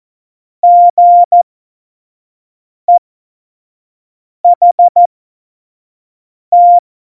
Morse code, heard as a keyed tone, about 14 wpm, spaced as for 5 wpm, 700 Hz. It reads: GEHT
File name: Morsekurs / Witze / Witz 14. November 2017, 25 F70